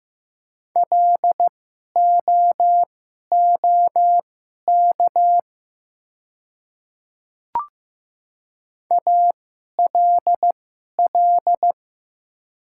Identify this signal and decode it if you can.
Morse code, heard as a keyed tone, about 15 wpm, spaced as for 11 wpm, 700 Hz. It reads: LOOK ALL
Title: 2x Character Spacing - Top 200 Words - 15wpm